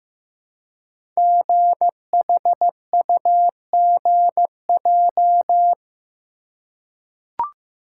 Morse code, heard as a keyed tone, about 15 wpm, 700 Hz. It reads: GHUGJ
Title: Code Groups - 5 Letters - 15wpm